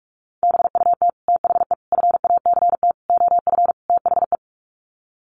Morse code, heard as a keyed tone, about 44 wpm, 700 Hz. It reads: BUT THE FACT OF THE
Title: Sets of 5 Words Encore2 45wpm